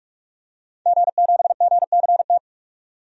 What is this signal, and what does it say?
Morse code, heard as a keyed tone, about 43 wpm, 700 Hz. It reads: G7GCT
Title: DXCC-100 Call Signs - 5 Characters - 45wpm